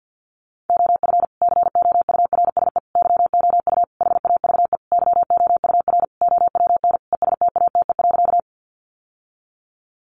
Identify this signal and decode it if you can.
Morse code, heard as a keyed tone, about 48 wpm, 700 Hz. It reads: OF COURSE YOU HAVE YOUR OWN ESTATE.
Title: Sentences from Top 1200 Words 50wpm